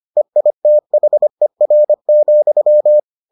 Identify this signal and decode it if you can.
Morse code, heard as a keyed tone, about 25 wpm, 600 Hz. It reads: EITHER,